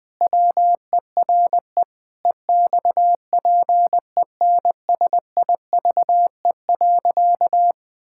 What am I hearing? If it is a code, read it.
Morse code, heard as a keyed tone, about 20 wpm, 700 Hz. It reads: WERE EXPENSIVE.
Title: Pangram Challenge v2 - 20wpm